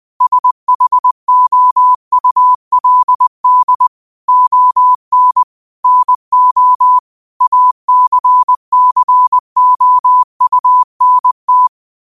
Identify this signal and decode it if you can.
Morse code, heard as a keyed tone, about 20 wpm, 1000 Hz. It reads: SHOULD ON NO ACCOUNT
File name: qotdfiles / 20wpm